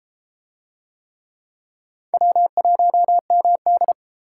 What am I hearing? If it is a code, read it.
Morse code, heard as a keyed tone, about 31 wpm, 700 Hz. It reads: W1MD